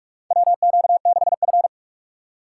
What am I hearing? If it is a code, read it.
Morse code, heard as a keyed tone, about 45 wpm, 700 Hz. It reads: WQ6F
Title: USA Call Signs - 4 Characters - 4x Character Spacing - 45wpm